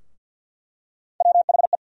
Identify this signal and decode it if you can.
Morse code, heard as a keyed tone, about 52 wpm, 700 Hz. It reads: WHE
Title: USA Call Sign Postfixes - 3 Letters 50wpm